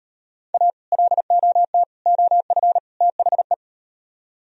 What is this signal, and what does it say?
Morse code, heard as a keyed tone, about 38 wpm, 700 Hz. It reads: A LOT OF THE